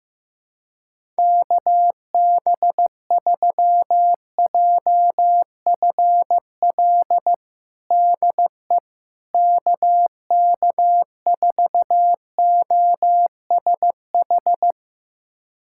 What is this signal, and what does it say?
Morse code, heard as a keyed tone, about 16 wpm, 700 Hz. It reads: KB3JFL DE KK4OSH
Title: USA Callsign DE Callsign 15wpm